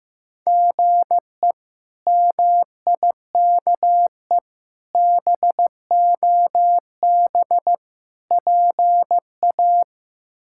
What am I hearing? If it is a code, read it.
Morse code, heard as a keyed tone, about 15 wpm, 700 Hz. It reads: GE MIKE BOB PA
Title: K1USN SST QSO Practice - 15wpm